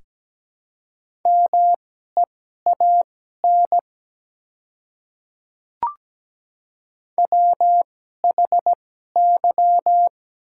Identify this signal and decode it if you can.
Morse code, heard as a keyed tone, about 17 wpm, spaced as for 12 wpm, 700 Hz. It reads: MEAN WHY
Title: 2x Character Spacing - Top 300 Words - 17wpm